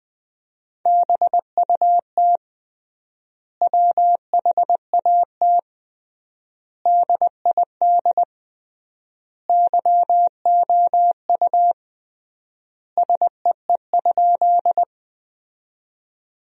Morse code, heard as a keyed tone, about 20 wpm, 700 Hz. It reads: BUT WHAT DID YOU SEE?
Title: 3x Word-Spacing - Sentences from Top 100 Words - 20wpm